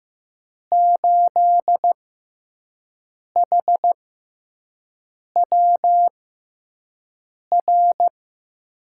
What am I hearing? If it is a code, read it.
Morse code, heard as a keyed tone, about 15 wpm, spaced as for 5 wpm, 700 Hz. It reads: 8HWR
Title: USA Call Signs - 6 Characters - 6x Character Spacing - 15wpm